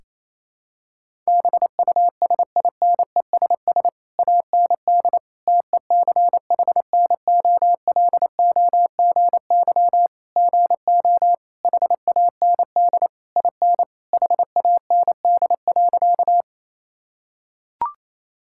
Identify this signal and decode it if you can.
Morse code, heard as a keyed tone, about 27 wpm, 700 Hz. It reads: BUSINESS AND TECHNOLOGY GO HAND IN HAND.
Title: Sentences from Top 1000 Words - Review Part 5 - 28wpm